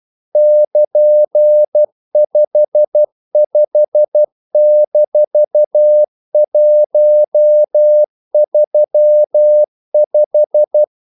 Morse code, heard as a keyed tone, about 12 wpm, 600 Hz. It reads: <KN>55-135
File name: CW / Practice / EnglishQuotations0312